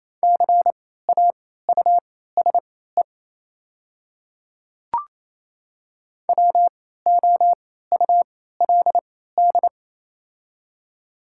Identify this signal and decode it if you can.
Morse code, heard as a keyed tone, about 28 wpm, spaced as for 16 wpm, 700 Hz. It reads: CAUSE WOULD
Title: 3x Character Spacing - Top 500 Words - 28wpm